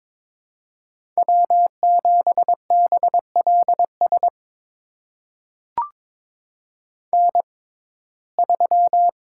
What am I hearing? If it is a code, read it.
Morse code, heard as a keyed tone, about 21 wpm, spaced as for 16 wpm, 700 Hz. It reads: W7BLS N3
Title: USA Call Signs - 5 Characters - 6x Character Spacing - 22wpm